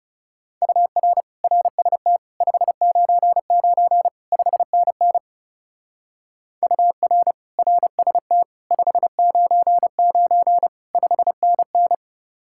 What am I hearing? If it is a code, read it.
Morse code, heard as a keyed tone, about 38 wpm, 700 Hz. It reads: UR RST 599 5NN UR RST 599 5NN